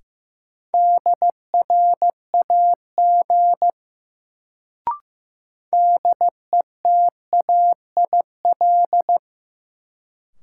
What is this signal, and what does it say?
Morse code, heard as a keyed tone, about 15 wpm, 700 Hz. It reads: DRAG DETAIL
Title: Sentences from Top 1500 Words - New Word Drill - 15wpm